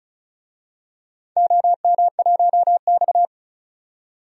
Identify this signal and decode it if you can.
Morse code, heard as a keyed tone, about 36 wpm, 700 Hz. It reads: OM1X
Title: DXCC-100 Call Signs - 4 Characters - 35wpm